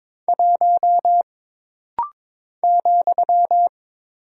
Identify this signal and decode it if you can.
Morse code, heard as a keyed tone, about 22 wpm, 700 Hz. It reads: 1 ,
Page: This rapid-fire timing is geared to push your ICR (Instant Character Recognition) skills.